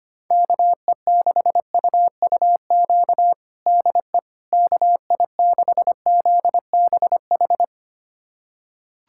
Morse code, heard as a keyed tone, about 25 wpm, 700 Hz. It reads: KE6UUQ DE KI6ZBH
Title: USA Callsign DE Callsign 25wpm